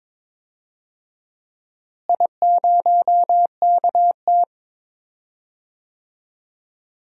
Morse code, heard as a keyed tone, about 22 wpm, 700 Hz. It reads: I0KT
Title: DXCC-100 Call Signs - 4 Characters - Repeated 3x - 22wpm